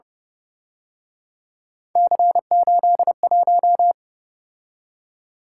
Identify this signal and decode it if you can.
Morse code, heard as a keyed tone, about 28 wpm, 700 Hz. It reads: C81